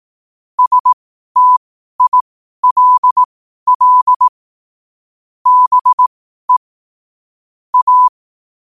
Morse code, heard as a keyed tone, about 18 wpm, spaced as for 12 wpm, 1000 Hz. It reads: STILL BE A
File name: qotdfiles / 12wpm